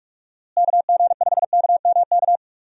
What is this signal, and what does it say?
Morse code, heard as a keyed tone, about 46 wpm, 700 Hz. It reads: KG5KMK
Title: USA Call Signs - 6 Characters - Repeated 2x - 45wpm